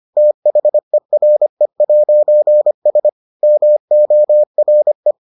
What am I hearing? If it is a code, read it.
Morse code, heard as a keyed tone, about 25 wpm, 600 Hz. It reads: THERE'S MORE